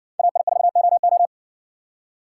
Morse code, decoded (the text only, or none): AI4OO